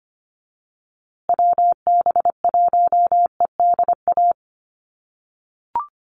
Morse code, heard as a keyed tone, about 25 wpm, 700 Hz. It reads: WB1EDA
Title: USA Call Signs Repeated 2x 25wpm